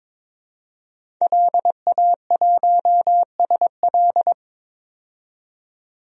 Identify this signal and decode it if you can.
Morse code, heard as a keyed tone, about 22 wpm, 700 Hz. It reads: LA1SL
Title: DXCC-100 Call Signs - 5 Characters - 22wpm